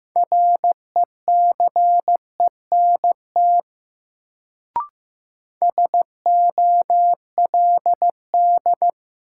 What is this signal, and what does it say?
Morse code, heard as a keyed tone, about 15 wpm, 700 Hz. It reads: RECENT SOLD
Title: Sentences from Top 1400 Words - New Word Drill Alt - 15wpm